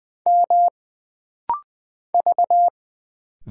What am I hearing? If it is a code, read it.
Morse code, heard as a keyed tone, about 20 wpm, 700 Hz. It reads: M V
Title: Single Letter-Number - Rapid-Fire 20wpm